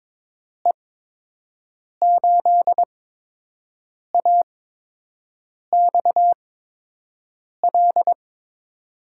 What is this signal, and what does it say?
Morse code, heard as a keyed tone, about 21 wpm, spaced as for 6 wpm, 700 Hz. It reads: E8AXL
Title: USA Call Signs - 8x Character Spacing - 22wpm